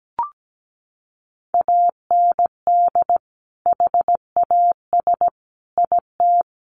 Morse code, heard as a keyed tone, about 17 wpm, 700 Hz. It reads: AND HAS IT
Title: Sentences from Top 100 Words 17wpm